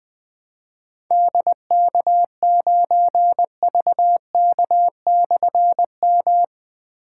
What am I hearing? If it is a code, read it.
Morse code, heard as a keyed tone, about 20 wpm, 700 Hz. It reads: DK9VK/M
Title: Difficult Call Signs 20wpm